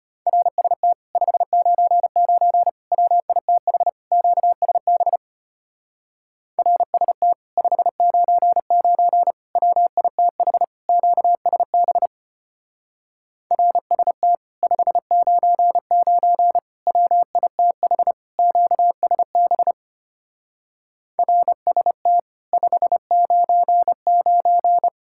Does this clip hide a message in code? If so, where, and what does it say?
Yes: Morse code from 0.3 to 24.9 s, heard as a keyed tone, about 33 wpm, 700 Hz. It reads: RST 599 WITH QSB RST 599 WITH QSB RST 599 WITH QSB RST 599